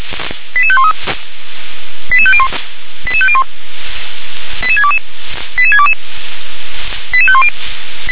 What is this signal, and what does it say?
Five-tone selective call (ZVEI), heard as quick runs of five tones, tones of about 70 ms. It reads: 88513 88501 88501 88622 88622 88510
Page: SDR/ZVEI